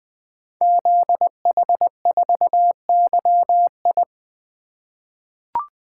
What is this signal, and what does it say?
Morse code, heard as a keyed tone, about 20 wpm, 700 Hz. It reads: ZH4YI
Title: Code Groups - 5 Letter:Numbers - 20wpm